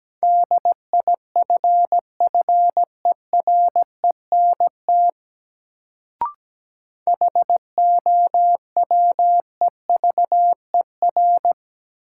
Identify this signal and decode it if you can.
Morse code, heard as a keyed tone, about 17 wpm, 700 Hz. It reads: DIFFERENT HOWEVER
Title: Sentences from Top 600 Words - New Word Drill - 17wpm